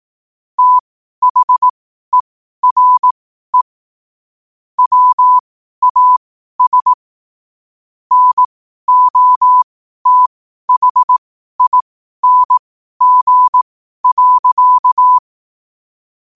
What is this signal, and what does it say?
Morse code, heard as a keyed tone, about 18 wpm, spaced as for 12 wpm, 1000 Hz. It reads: THERE WAS NOTHING.
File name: qotdfiles / 12wpm